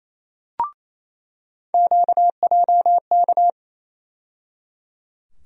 Morse code, heard as a keyed tone, about 28 wpm, 700 Hz. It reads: QJK
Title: USA Call Sign Postfixes - 3 Letters 28wpm